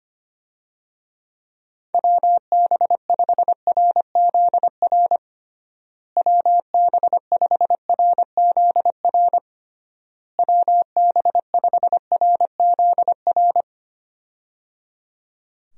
Morse code, heard as a keyed tone, about 25 wpm, 700 Hz. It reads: WB5RZR WB5RZR WB5RZR
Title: USA Call Signs - 6 Characters - Repeated 3x - 25wpm